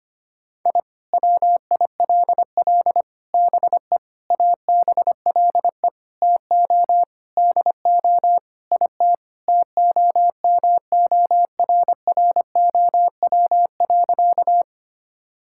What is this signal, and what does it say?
Morse code, heard as a keyed tone, about 25 wpm, 700 Hz. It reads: I WILL BE ABLE TO DO IT TOMORROW.
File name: practice / 25wpm / Sentences with 4-Word Phrases - 25wpm